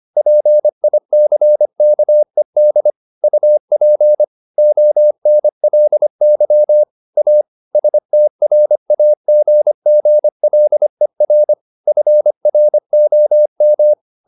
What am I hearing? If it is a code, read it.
Morse code, heard as a keyed tone, about 25 wpm, 600 Hz. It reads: PICKED UP ONLY A STRAGGLER FROM
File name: the old man and the sea part 1 25wpm